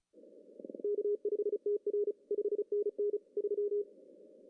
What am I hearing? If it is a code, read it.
Morse code, heard as a keyed tone, about 33 wpm, 410 Hz. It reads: K5TR 5NN 3
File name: CW / cqww-cw-k5tr